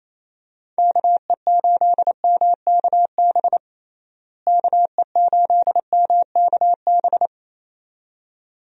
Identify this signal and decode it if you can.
Morse code, heard as a keyed tone, about 28 wpm, 700 Hz. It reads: KE8MKB KE8MKB
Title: USA Call Signs - 6 Characters - Repeated 2x - 28wpm